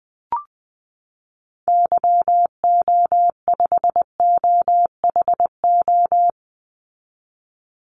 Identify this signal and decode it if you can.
Morse code, heard as a keyed tone, about 20 wpm, 700 Hz. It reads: YO5OHO
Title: Call Signs Repeated 3x - 20wpm